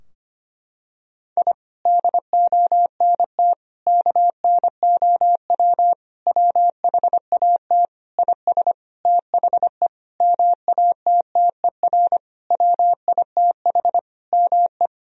Morse code, heard as a keyed tone, about 25 wpm, 700 Hz. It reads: I DONT KNOW WHAT IS THE MATTER WITH ME
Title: Sentences from Top 1000 Words - Review Part 4 - 25wpm